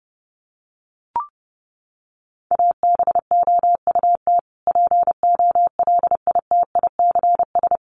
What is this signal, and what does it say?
Morse code, heard as a keyed tone, about 30 wpm, 700 Hz. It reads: ABOUT POLITICS
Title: Sentences from Top 1000 Words 30wpm